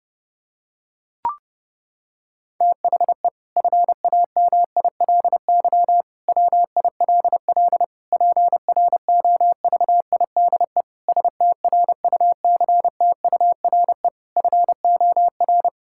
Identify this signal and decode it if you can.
Morse code, heard as a keyed tone, about 32 wpm, 700 Hz. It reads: THE FAMILY WILL PROVIDE STRUCTURE FOR
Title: Sentences from Top 1000 Words - Review Part 6 - 30wpm